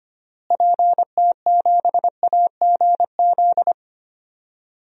Morse code, heard as a keyed tone, about 25 wpm, 700 Hz. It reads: PT7AGZ
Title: DXCC-100 Call Signs - 6 Characters - Repeated 3x - 25wpm